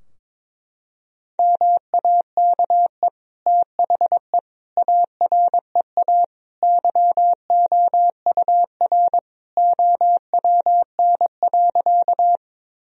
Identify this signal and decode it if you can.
Morse code, heard as a keyed tone, about 22 wpm, 700 Hz. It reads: MAKE THE AREA YOUR OWN.
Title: Sentences from Top 1000 Words - Review Part 2 - 22wpm